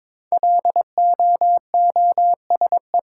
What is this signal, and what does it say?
Morse code, heard as a keyed tone, about 22 wpm, 700 Hz. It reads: LOOSE
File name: 5 Letter Words - First 3 Letters Given - 22wpm